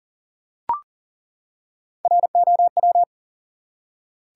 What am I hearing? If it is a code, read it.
Morse code, heard as a keyed tone, about 40 wpm, 700 Hz. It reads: ROW
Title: Three Letter Words - 40wpm